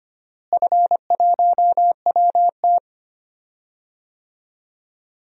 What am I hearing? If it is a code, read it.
Morse code, heard as a keyed tone, about 25 wpm, 700 Hz. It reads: F1WT
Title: DXCC-100 Call Signs - 4 Characters - Repeated 3x - 25wpm